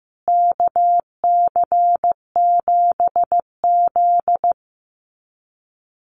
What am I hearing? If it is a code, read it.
Morse code, heard as a keyed tone, about 14 wpm, 700 Hz. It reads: KC7Z
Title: USA Call Signs Repeated 3x 15wpm